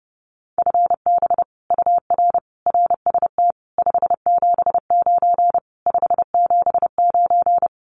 Morse code, heard as a keyed tone, about 30 wpm, 700 Hz. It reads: FB UR RST 579 579
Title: Common QSO Phrases Encore 30wpm